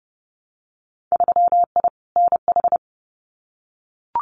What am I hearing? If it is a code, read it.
Morse code, heard as a keyed tone, about 30 wpm, 700 Hz. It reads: 3I NH
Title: Winter Field Day Exchanges Encore 30wpm